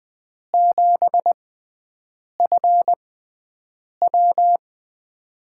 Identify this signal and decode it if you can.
Morse code, heard as a keyed tone, about 20 wpm, spaced as for 7 wpm, 700 Hz. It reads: 7FW
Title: USA Call Signs - 4 Characters - 6x Character Spacing - 20wpm